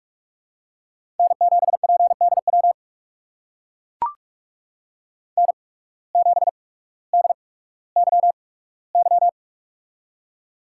Morse code, heard as a keyed tone, about 44 wpm, spaced as for 19 wpm, 700 Hz. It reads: N7PDW N7DYY
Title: USA Call Signs - 5 Characters - 8x Character Spacing - 45wpm